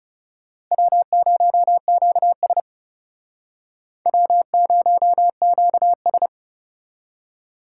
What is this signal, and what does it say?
Morse code, heard as a keyed tone, about 38 wpm, 700 Hz. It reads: W0QS W0QS